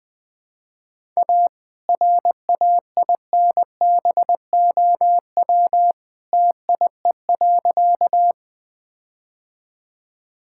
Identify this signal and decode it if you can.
Morse code, heard as a keyed tone, about 21 wpm, 700 Hz. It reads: A RAINBOW TIE.
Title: All-Age Appropriate Jokes - 20wpm